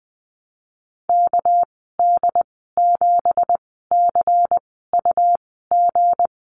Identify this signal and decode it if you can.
Morse code, heard as a keyed tone, about 20 wpm, spaced as for 14 wpm, 700 Hz. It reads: KD7CUG
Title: 231 Lesson 37g - Letter x - Call Signs 20wpm